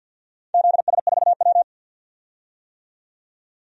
Morse code, heard as a keyed tone, about 51 wpm, 700 Hz. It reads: ZS4W
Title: DXCC-100 Call Signs - 4 Characters - 50wpm